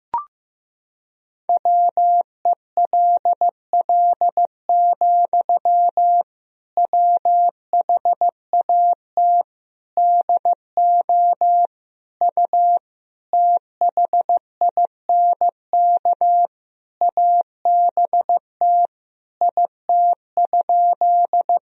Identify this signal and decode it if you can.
Morse code, heard as a keyed tone, about 15 wpm, 700 Hz. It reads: WELL, WHAT DO U THINK ABT IT?
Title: Sentences from Top 200 Words and Abbreviations 15wpm